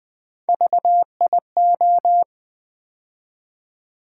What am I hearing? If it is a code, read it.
Morse code, heard as a keyed tone, about 20 wpm, 700 Hz. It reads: VIO
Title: Build A Word - 3 Letters At A Time - 9-Letter Words - 20wpm